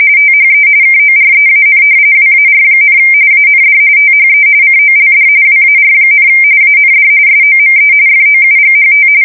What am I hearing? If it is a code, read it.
RTTY, about 45.45 baud. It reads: THE KOREAN WAR, AND THE COLD WAR, AND SOME FOUGHT
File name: fsk170 / battleshipnj